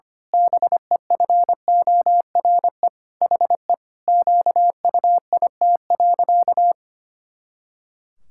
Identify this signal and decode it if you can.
Morse code, heard as a keyed tone, about 25 wpm, 700 Hz. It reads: BEFORE HE QUIT.
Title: Pangram Challenge v2 - 25wpm